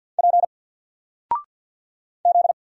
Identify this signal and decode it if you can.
Morse code, heard as a keyed tone, about 48 wpm, 700 Hz. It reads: P Z
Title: Single Character (Letter:Number:Punctuation:Prosign) 50wpm